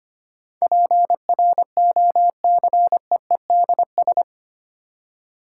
Morse code, heard as a keyed tone, about 25 wpm, 700 Hz. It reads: PROCEEDS
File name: Eight Letter Words - 25wpm